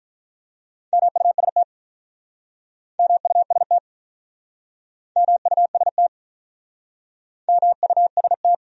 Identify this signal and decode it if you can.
Morse code, heard as a keyed tone, about 48 wpm, 700 Hz. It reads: MUST MUST MUST MUST